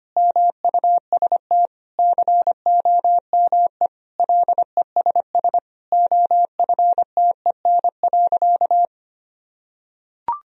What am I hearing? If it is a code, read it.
Morse code, heard as a keyed tone, about 25 wpm, 700 Hz. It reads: MUST COME LESS OFTEN.
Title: Sentences from Top 1000 Words - Review Part 2 - 25wpm